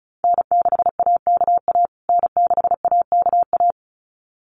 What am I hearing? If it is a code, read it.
Morse code, heard as a keyed tone, about 34 wpm, 700 Hz. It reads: N6AKA N6AKA
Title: USA Call Signs Repeated 2x 35wpm